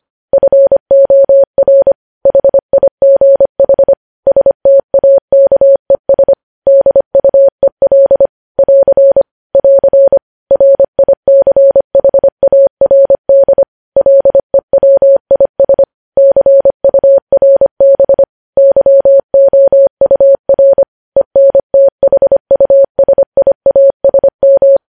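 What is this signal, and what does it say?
Morse code, heard as a keyed tone, about 25 wpm, 550 Hz. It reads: FOR HIGH STAKES DUEL <AR> <AR> RICHARD LEWIS CURB YOUR ENTHUSIASM
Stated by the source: Morse news headlines sent at 25 wpm.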